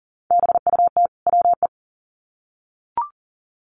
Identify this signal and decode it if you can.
Morse code, heard as a keyed tone, about 40 wpm, 700 Hz. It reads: BUT WE
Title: Sets of 2 Words 40wpm